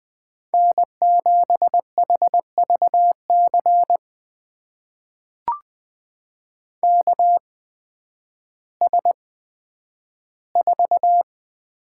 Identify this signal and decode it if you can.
Morse code, heard as a keyed tone, about 20 wpm, spaced as for 11 wpm, 700 Hz. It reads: N7HVC KS4
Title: USA Call Signs - 5 Characters - 8x Character Spacing - 20wpm